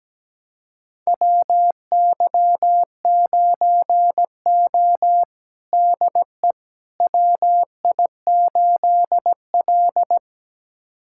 Morse code, heard as a keyed tone, about 17 wpm, 700 Hz. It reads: WY9O DE WI8L
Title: USA Callsign DE Callsign - 4 Characters - 17wpm